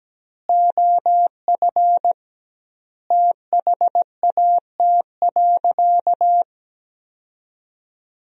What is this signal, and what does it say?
Morse code, heard as a keyed tone, about 17 wpm, 700 Hz. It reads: OF THAT.
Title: 2x Word-Spacing - Sentences from Top 300 Words - 17wpm